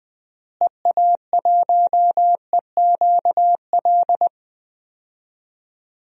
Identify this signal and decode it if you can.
Morse code, heard as a keyed tone, about 20 wpm, 690 Hz. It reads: EA1EQL
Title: DXCC-100 Call Signs - 6 Characters - Repeated 2x - 20wpm